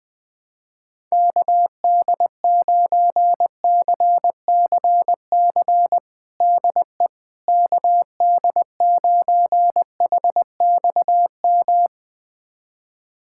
Morse code, heard as a keyed tone, about 20 wpm, 700 Hz. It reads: KD9CCC DE KD9HXM
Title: USA Callsign DE Callsign - 6 Characters - 20wpm